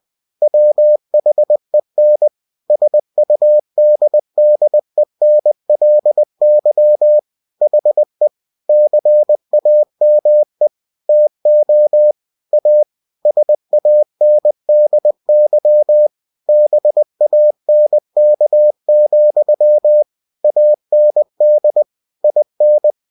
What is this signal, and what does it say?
Morse code, heard as a keyed tone, about 20 wpm, 600 Hz. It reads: WHEN SUDDENLY HE CAME TO A SANDY BANK, AND IN
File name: cw-books / winnie-the-pooh / winnie-20wpm-page-0013